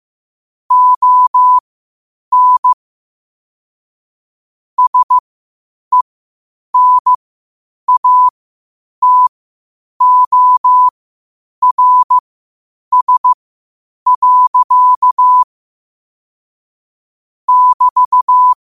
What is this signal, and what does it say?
Morse code, heard as a keyed tone, about 15 wpm, spaced as for 8 wpm, 1000 Hz. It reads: ON SENATORS. <BT>